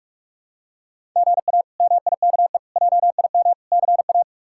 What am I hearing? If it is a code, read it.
Morse code, heard as a keyed tone, about 44 wpm, 700 Hz. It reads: GA MIKE JIM CA
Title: K1USN SST QSO Practice - 45wpm